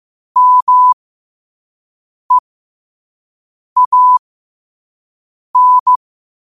Morse code, heard as a keyed tone, about 15 wpm, spaced as for 5 wpm, 1000 Hz. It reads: MEAN
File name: qotdfiles / 5wpm